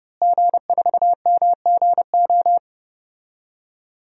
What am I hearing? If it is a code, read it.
Morse code, heard as a keyed tone, about 30 wpm, 700 Hz. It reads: G4MGO
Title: DXCC-100 Call Signs - 5 Characters - Repeated 2x - 30wpm